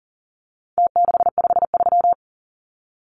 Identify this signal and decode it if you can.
Morse code, heard as a keyed tone, about 40 wpm, 700 Hz. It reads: T653
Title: Numbers - Contest Style - 4-Digit 40wpm